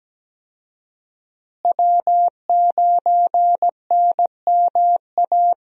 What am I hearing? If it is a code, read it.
Morse code, heard as a keyed tone, about 17 wpm, 700 Hz. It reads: W9NMA
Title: USA Call Signs - 5 Characters - Repeated 2x - 17wpm